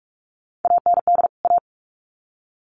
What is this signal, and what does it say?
Morse code, heard as a keyed tone, about 44 wpm, 700 Hz. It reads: AND A
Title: Sets of 2 Words 45wpm